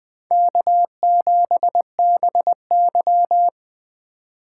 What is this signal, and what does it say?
Morse code, heard as a keyed tone, about 20 wpm, 700 Hz. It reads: K7BY
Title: Speed Racing - USA 1x2 Call Signs 17wpm